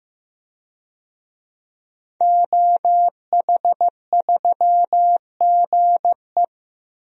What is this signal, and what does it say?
Morse code, heard as a keyed tone, about 15 wpm, 700 Hz. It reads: OH3GE
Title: DXCC-100 Call Signs - 5 Characters - 15wpm